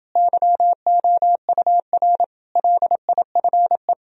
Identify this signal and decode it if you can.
Morse code, heard as a keyed tone, about 31 wpm, 700 Hz. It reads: YOUR LIFE